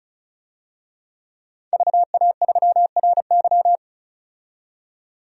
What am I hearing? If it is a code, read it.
Morse code, heard as a keyed tone, about 36 wpm, 700 Hz. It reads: VA3RY
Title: DXCC-100 Call Signs - 5 Characters - 35wpm